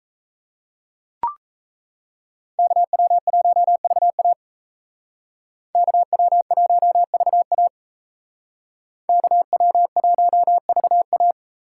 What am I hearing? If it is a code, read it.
Morse code, heard as a keyed tone, about 35 wpm, 700 Hz. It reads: KW1VA KW1VA KW1VA